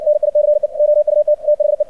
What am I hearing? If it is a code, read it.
Morse code, heard as a keyed tone, about 60 wpm, 600 Hz, with radio noise. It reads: NICE OUT THE
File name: archived / CRSnet